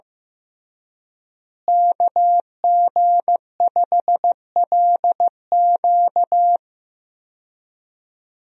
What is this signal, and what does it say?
Morse code, heard as a keyed tone, about 15 wpm, 700 Hz. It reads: KG5LQ
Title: USA Call Signs - 5 Characters - Repeated 3x - 15wpm